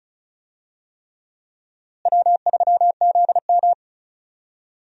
Morse code, heard as a keyed tone, about 36 wpm, 700 Hz. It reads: W3ZM
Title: USA Call Signs - 4 Characters - Repeated 3x - 35wpm